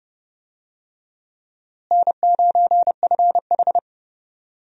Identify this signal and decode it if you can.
Morse code, heard as a keyed tone, about 30 wpm, 700 Hz. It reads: N9FH
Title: USA Call Signs - 4 Characters - Repeated 3x - 30wpm